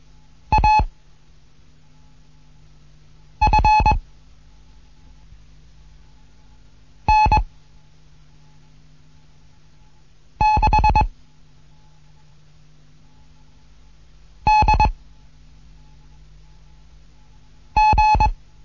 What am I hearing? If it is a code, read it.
Morse code, heard as a keyed tone, about 21 wpm, 860 Hz. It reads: A F N 6 D G
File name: Morsen / Kurs A 15